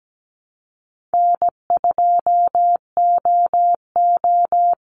Morse code, heard as a keyed tone, about 17 wpm, 700 Hz. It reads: N2OO
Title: Easy Call Signs Repeated 3x - 17wpm